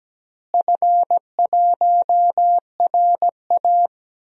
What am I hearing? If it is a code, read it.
Morse code, heard as a keyed tone, about 17 wpm, 700 Hz. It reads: F1RA
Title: DXCC-100 Call Signs - 4 Characters - Repeated 2x - 17wpm